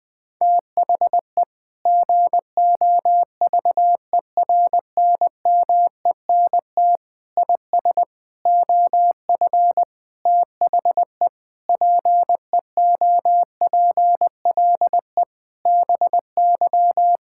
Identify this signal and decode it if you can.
Morse code, heard as a keyed tone, about 20 wpm, 700 Hz. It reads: THE GOVERNMENT IS OF THE PEOPLE BY
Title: Sentences with 9-Word Phrases - Spoken Beginning - 20wpm